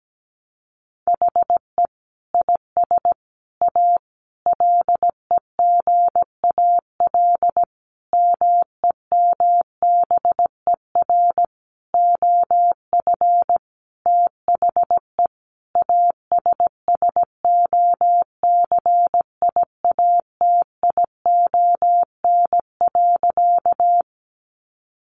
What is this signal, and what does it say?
Morse code, heard as a keyed tone, about 17 wpm, 700 Hz. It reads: HE IS A LEGAL MEMBER OF THE ASSOCIATION.
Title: Sentences from Top 1200 Words 17wpm